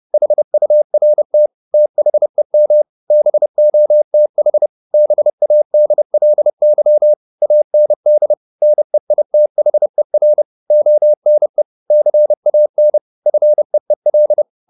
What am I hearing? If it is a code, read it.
Morse code, heard as a keyed tone, about 30 wpm, 600 Hz. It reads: HURT THEM BOTH BADLY AND NEITHER ONE CAN FEEL